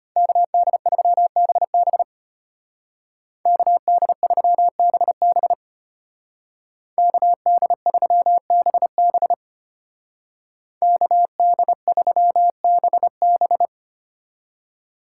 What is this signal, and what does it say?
Morse code, heard as a keyed tone, about 30 wpm, 700 Hz. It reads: KD3BB KD3BB KD3BB KD3BB
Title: Speed Racing - USA 2x2 Call Signs 25wpm